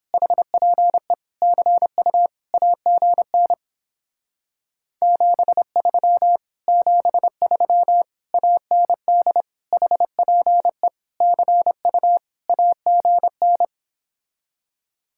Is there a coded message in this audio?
Yes: Morse code from 0.1 to 13.7 s, heard as a keyed tone, about 29 wpm, 700 Hz. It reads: HPE CU AGN 73 73 AND HPE CU AGN